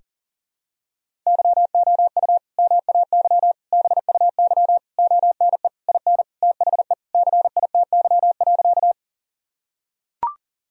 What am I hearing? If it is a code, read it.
Morse code, heard as a keyed tone, about 40 wpm, 700 Hz. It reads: YOU MAY BUY ONE IN THE CITY.
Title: Sentences from Top 1000 Words - Review Part 4 - 40wpm